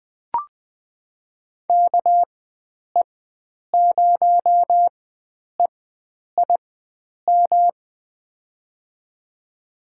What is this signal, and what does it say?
Morse code, heard as a keyed tone, about 21 wpm, spaced as for 9 wpm, 700 Hz. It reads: KE0EIM
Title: USA Call Signs - 6 Characters - 4x Character Spacing - 20wpm